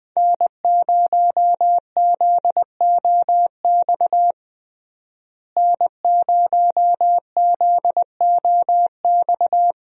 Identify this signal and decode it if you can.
Morse code, heard as a keyed tone, about 20 wpm, 700 Hz. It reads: N0ZOX N0ZOX
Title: USA Call Signs - 5 Characters - Repeated 2x - 20wpm